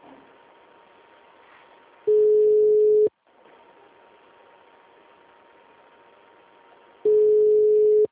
Ringback tone (Europe, 425 Hz).